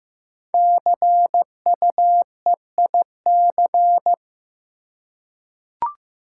Morse code, heard as a keyed tone, about 15 wpm, 700 Hz. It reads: CUEIC
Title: Code Groups - 5 Letter:Numbers - 15wpm